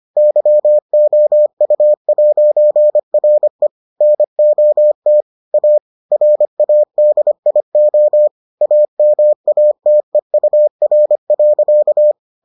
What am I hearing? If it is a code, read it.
Morse code, heard as a keyed tone, about 25 wpm, 600 Hz. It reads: YOU'RE NOT A RADIO AMATEUR.